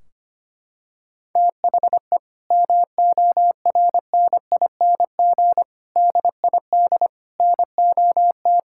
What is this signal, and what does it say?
Morse code, heard as a keyed tone, about 25 wpm, 700 Hz. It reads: THE MORNING DID NOT
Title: Sentences from Top 1600 Words 25wpm